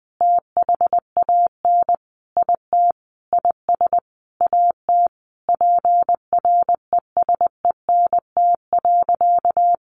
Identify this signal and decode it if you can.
Morse code, heard as a keyed tone, about 20 wpm, 700 Hz. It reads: THAN IT IS AT PRESENT.
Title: Sentences from Top 1200 Words 20wpm